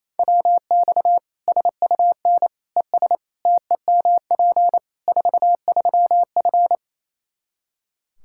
Morse code, heard as a keyed tone, about 28 wpm, 700 Hz. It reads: WX SUN ES TEMP 43F
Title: Practice QSO - Exchange 2 - Info - 28wpm